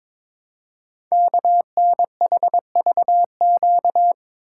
Morse code, heard as a keyed tone, about 22 wpm, 700 Hz. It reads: KNHVQ
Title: Code Groups - 5 Letters - 22wpm